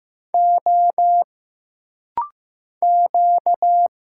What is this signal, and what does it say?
Morse code, heard as a keyed tone, about 15 wpm, 700 Hz. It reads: O Q
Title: Single Letter-Number Alt - Rapid-Fire 15wpm